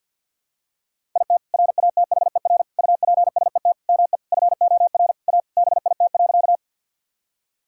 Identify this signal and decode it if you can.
Morse code, heard as a keyed tone, about 52 wpm, 700 Hz. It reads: IT RATHER UPSET ME FOR A BIT.
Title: Sentences from Top 1000 Words - Review Part 5 - 50wpm